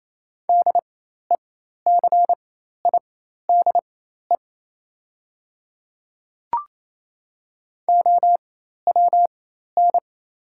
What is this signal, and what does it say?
Morse code, heard as a keyed tone, about 28 wpm, spaced as for 13 wpm, 700 Hz. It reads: DECIDE OWN
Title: 4x Character Spacing - Top 500 Words - 28wpm